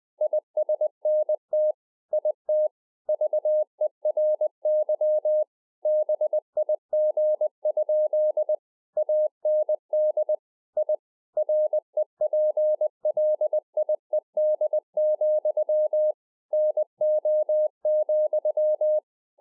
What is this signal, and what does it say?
Morse code, heard as a keyed tone, about 20 wpm, 610 Hz. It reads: ISNT IT VERY BIG? AND I REPLIED, NO,